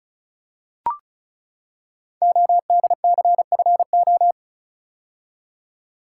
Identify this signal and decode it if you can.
Morse code, heard as a keyed tone, about 36 wpm, 690 Hz. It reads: ODCFO
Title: Code Groups - 5 Letters - 35wpm